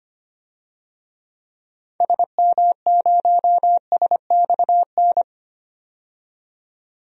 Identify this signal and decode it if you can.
Morse code, heard as a keyed tone, about 25 wpm, 700 Hz. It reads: SM0SXN
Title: DXCC-100 Call Signs - 6 Characters - 25wpm